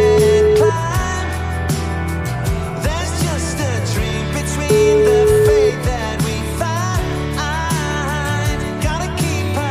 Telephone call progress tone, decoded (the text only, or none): ringback tone